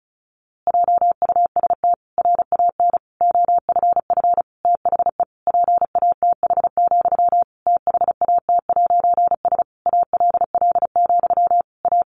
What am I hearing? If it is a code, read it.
Morse code, heard as a keyed tone, about 34 wpm, 700 Hz. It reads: JUST RAN OFF THE PATH, THAT'S ALL, A
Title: A Sound Of Thunder 35wpm